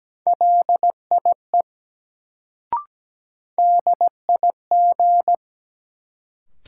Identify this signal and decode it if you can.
Morse code, heard as a keyed tone, about 17 wpm, 700 Hz. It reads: LIE DIG
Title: Sentences from Top 1600 Words - New Word Drill - 17wpm